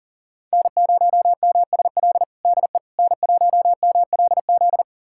Morse code, heard as a keyed tone, about 40 wpm, 700 Hz. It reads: N0MSL DE N1MLZ
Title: USA Callsign DE Callsign - 5 Characters - 40wpm